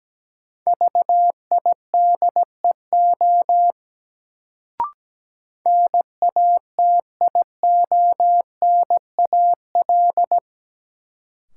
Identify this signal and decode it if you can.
Morse code, heard as a keyed tone, about 17 wpm, 700 Hz. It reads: VIDEO NATIONAL
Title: Sentences from Top 1000 Words - Review Part 4 - New Word Drill - 17wpm